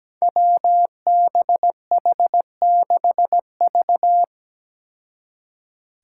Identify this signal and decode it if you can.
Morse code, heard as a keyed tone, about 17 wpm, 700 Hz. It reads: WBH6V
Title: Five Challenging Letter-Number Combinations 17wpm